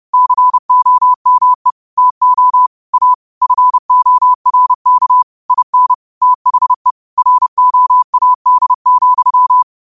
Morse code, heard as a keyed tone, about 30 wpm, 1000 Hz. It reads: COME TO A FORK IN THE ROAD,
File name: qotdfiles / 30wpm